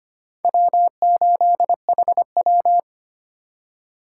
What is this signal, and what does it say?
Morse code, heard as a keyed tone, about 25 wpm, 700 Hz. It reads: W8HW
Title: K1USN SST QSO Practice - 25wpm